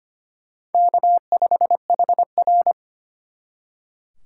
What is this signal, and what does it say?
Morse code, heard as a keyed tone, about 32 wpm, 700 Hz. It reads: K5HR